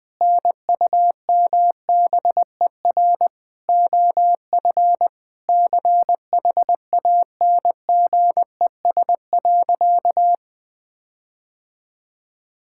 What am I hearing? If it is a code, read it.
Morse code, heard as a keyed tone, about 20 wpm, 700 Hz. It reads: NUMBER OF CHANGES.
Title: Sentences with 6-Word Phrases - 20wpm